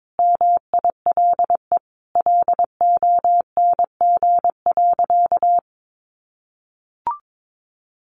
Morse code, heard as a keyed tone, about 22 wpm, 700 Hz. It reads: MILE LONG.
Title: Sentences from Top 500 Words 22wpm